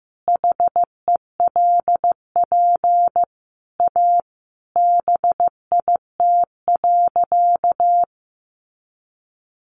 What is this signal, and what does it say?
Morse code, heard as a keyed tone, about 15 wpm, 700 Hz. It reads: HELP A BIT.
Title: Sentences from Top 900 Words 15wpm